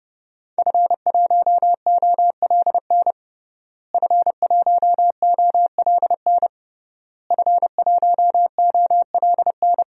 Morse code, heard as a keyed tone, about 32 wpm, 700 Hz. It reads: F1OLN F1OLN F1OLN
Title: DXCC-100 Call Signs - 5 Characters - Repeated 3x - 30wpm